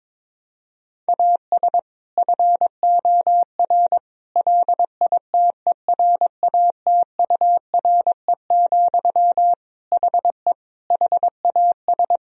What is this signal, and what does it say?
Morse code, heard as a keyed tone, about 22 wpm, 700 Hz. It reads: AS FOR LITERATURE, HE HAS
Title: Sentences from Top 1000 Words - Review Part 3 - 22wpm